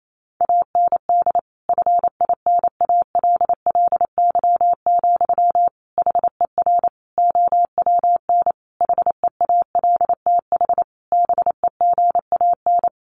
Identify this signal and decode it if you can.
Morse code, heard as a keyed tone, about 28 wpm, 700 Hz. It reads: AND FINALLY, HER OWN HEALTH BEGAN
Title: Sentences from Top 1100 Words 28wpm